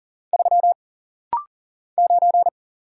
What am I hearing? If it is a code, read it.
Morse code, heard as a keyed tone, about 40 wpm, 700 Hz. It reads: 3 9
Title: Single Letter-Number - Rapid-Fire 40wpm